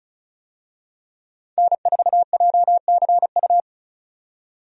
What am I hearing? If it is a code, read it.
Morse code, heard as a keyed tone, about 36 wpm, 700 Hz. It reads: N4JCU
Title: USA Call Signs - 5 Characters - 35wpm